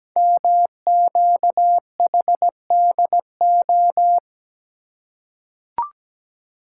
Morse code, heard as a keyed tone, about 17 wpm, 700 Hz. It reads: MQHDO
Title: Code Groups - 5 Letters - 17wpm